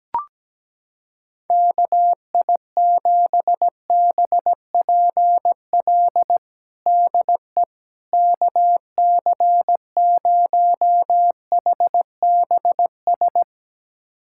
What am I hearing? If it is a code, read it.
Morse code, heard as a keyed tone, about 17 wpm, 700 Hz. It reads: KI7BPL DE KC0HBS
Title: USA Callsign DE Callsign - 6 Characters - 17wpm